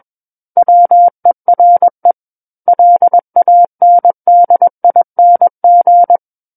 Morse code, heard as a keyed tone, about 21 wpm, 700 Hz. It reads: WERE LANDING
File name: Book One / 12-Chapter 12 - 21 WPM